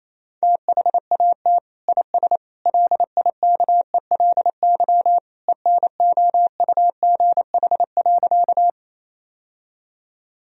Morse code, heard as a keyed tone, about 28 wpm, 700 Hz. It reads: THAT IS LIKELY ENOUGH.
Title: Sentences from Top 1000 Words - Review Part 4 - 28wpm